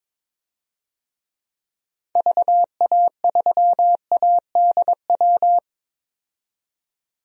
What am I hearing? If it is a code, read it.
Morse code, heard as a keyed tone, about 22 wpm, 700 Hz. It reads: VA3ADW
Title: DXCC-100 Call Signs - 6 Characters - 22wpm